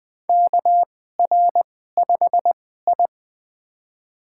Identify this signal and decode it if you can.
Morse code, heard as a keyed tone, about 20 wpm, spaced as for 15 wpm, 700 Hz. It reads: KR5I
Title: USA Call Signs - 4 Characters - 2x Character Spacing - 20wpm